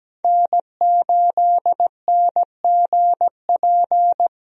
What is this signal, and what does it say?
Morse code, heard as a keyed tone, about 17 wpm, 700 Hz. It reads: N8NGP
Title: USA Call Signs - 5 Characters - Repeated 2x - 17wpm